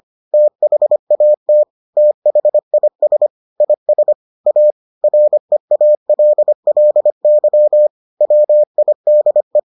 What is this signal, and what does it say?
Morse code, heard as a keyed tone, about 25 wpm, 600 Hz. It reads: THAT THIS IS A REALLY WIDE